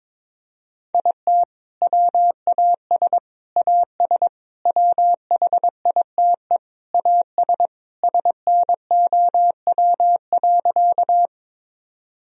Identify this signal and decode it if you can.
Morse code, heard as a keyed tone, about 22 wpm, 700 Hz. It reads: IT WAS AS WHITE AS SNOW.
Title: Sentences from Top 1000 Words - Review Part 3 - 22wpm